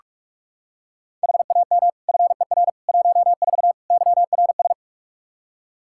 Morse code, heard as a keyed tone, about 46 wpm, 700 Hz. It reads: HAM FER 14 YRS
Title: K1USN SST QSO Practice 45wpm